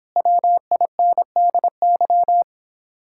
Morse code, heard as a keyed tone, about 26 wpm, 700 Hz. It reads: WINDY